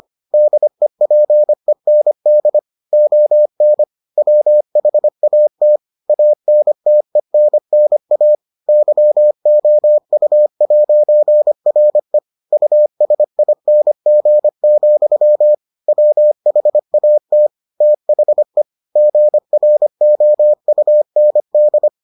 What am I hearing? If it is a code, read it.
Morse code, heard as a keyed tone, about 25 wpm, 600 Hz. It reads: DEPEND ON WHAT ANTENNA YOU'RE USING, WHAT THE GROUND